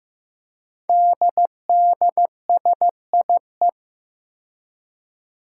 Morse code, heard as a keyed tone, about 15 wpm, 700 Hz. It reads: DDSIE
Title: Five Challenging Letter-Number Combinations 15wpm